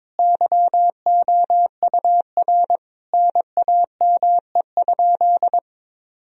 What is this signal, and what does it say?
Morse code, heard as a keyed tone, about 22 wpm, 700 Hz. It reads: YOUR NAME?
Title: Sentences from Top 1000 Words - Review Part 3 - 22wpm